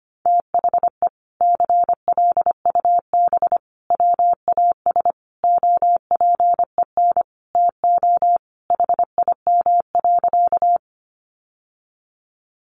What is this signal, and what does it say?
Morse code, heard as a keyed tone, about 25 wpm, 700 Hz. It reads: THE CLUB WAS OPEN TO HIM.
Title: Sentences from Top 1200 Words 25wpm